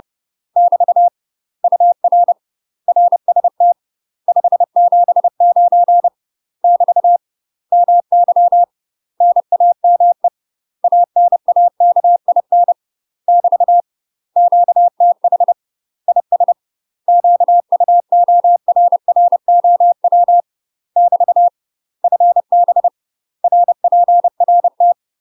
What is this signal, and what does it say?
Morse code, heard as a keyed tone, about 30 wpm, 700 Hz. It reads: <BT> UR RST 579 <BT> MY NAME ANAKIN <BT> QTH IS QUORROW <BT> FB RPRT
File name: UK MASTER QSOs TRAINING / 30WPM / UK MASTER QSOs 0042